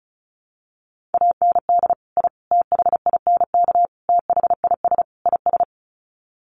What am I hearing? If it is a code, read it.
Morse code, heard as a keyed tone, about 34 wpm, 700 Hz. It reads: AND I THINK THIS IS
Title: Sets of 5 Words Encore 35wpm